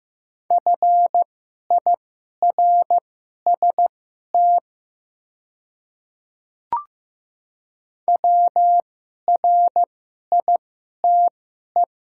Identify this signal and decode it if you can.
Morse code, heard as a keyed tone, about 15 wpm, spaced as for 11 wpm, 700 Hz. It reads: FIRST WRITE
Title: 2x Character Spacing - Top 300 Words - 15wpm